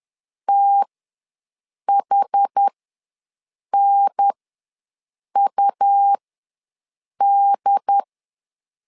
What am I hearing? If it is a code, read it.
Morse code, heard as a keyed tone, about 10 wpm, spaced as for 6 wpm, 800 Hz. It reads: THNUD
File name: LEZIONI / 4 Q ws 10wpm 2306